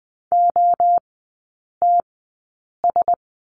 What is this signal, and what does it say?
Morse code, heard as a keyed tone, about 20 wpm, spaced as for 8 wpm, 700 Hz. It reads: OTS
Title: 074 Lesson 13e - Letter L - Three Characters 20wpm